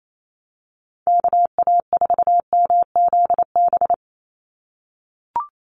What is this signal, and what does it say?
Morse code, heard as a keyed tone, about 28 wpm, 700 Hz. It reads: KA4MZB
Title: USA Call Signs 28wpm